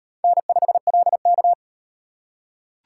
Morse code, heard as a keyed tone, about 38 wpm, 700 Hz. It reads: N5LK